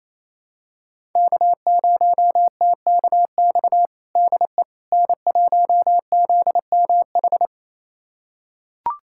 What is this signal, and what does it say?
Morse code, heard as a keyed tone, about 28 wpm, 700 Hz. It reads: K0TKX DE N1ZMH
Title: USA Callsign DE Callsign - 5 Characters - 28wpm